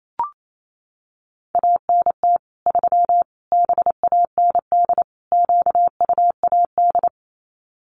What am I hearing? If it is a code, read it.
Morse code, heard as a keyed tone, about 28 wpm, 700 Hz. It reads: ANT 3 BAND QUAD
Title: Common QSO Phrases 28wpm